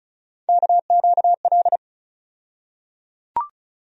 Morse code, heard as a keyed tone, about 34 wpm, 700 Hz. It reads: KQL
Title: USA Call Sign Postfixes - 3 Letters 35wpm